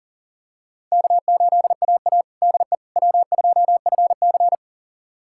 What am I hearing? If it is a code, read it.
Morse code, heard as a keyed tone, about 40 wpm, 700 Hz. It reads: K8AA DE W2FC
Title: USA Callsign DE Callsign - 4 Characters - 40wpm